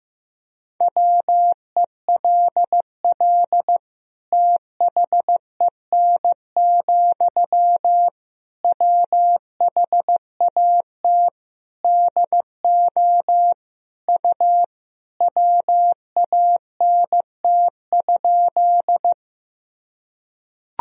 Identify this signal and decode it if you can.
Morse code, heard as a keyed tone, about 15 wpm, 700 Hz. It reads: WELL THEN, WHAT DO U WANT?
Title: Sentences from Top 200 Words and Abbreviations 15wpm